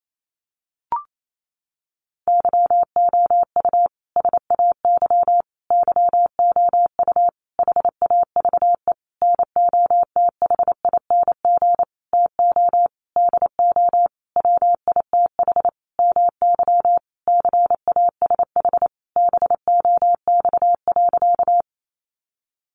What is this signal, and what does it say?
Morse code, heard as a keyed tone, about 28 wpm, 700 Hz. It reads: YOU SAY YOU HAVE NOTHING TO DO WITH MY CASH BOX.
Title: Sentences from Top 700 Words 28wpm